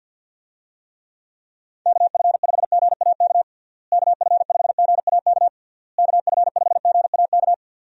Morse code, heard as a keyed tone, about 52 wpm, 700 Hz. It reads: KF5GAK KF5GAK KF5GAK
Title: USA Call Signs - 6 Characters - Repeated 3x - 50wpm